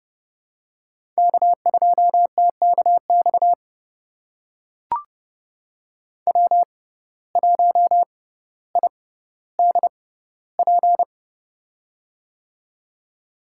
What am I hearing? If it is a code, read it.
Morse code, heard as a keyed tone, about 30 wpm, spaced as for 15 wpm, 700 Hz. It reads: K2TKX W1IDP
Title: USA Call Signs - 5 Characters - 6x Character Spacing - 30wpm